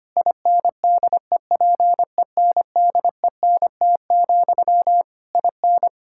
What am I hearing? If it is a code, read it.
Morse code, heard as a keyed tone, about 25 wpm, 700 Hz. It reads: INDEPENDENT, IN
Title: Sentences from Top 1000 Words - Review Part 6 - 25wpm